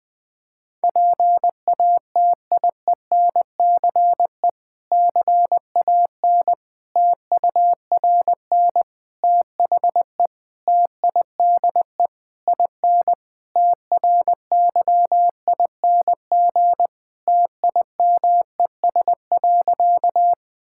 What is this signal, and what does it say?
Morse code, heard as a keyed tone, about 21 wpm, 700 Hz. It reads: PATIENCE CAN TURN THE TIDE IN TRYING TIMES.
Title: Difficult Sentences with Hints - 20wpm